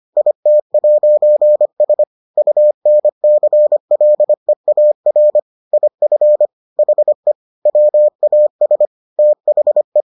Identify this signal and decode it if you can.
Morse code, heard as a keyed tone, about 25 wpm, 600 Hz. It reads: IT'S UNCLEAR IF HE WAS THE